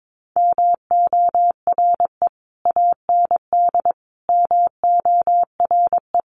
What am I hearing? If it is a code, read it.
Morse code, heard as a keyed tone, about 22 wpm, 700 Hz. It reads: MORE AND MORE
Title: Sets of 3 Words 22wpm